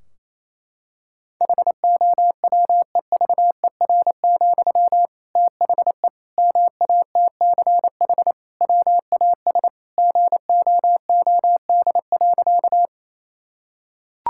Morse code, heard as a keyed tone, about 27 wpm, 700 Hz. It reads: HOWEVER, THE MATCH WAS GOOD.
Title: Sentences from Top 1600 Words 28wpm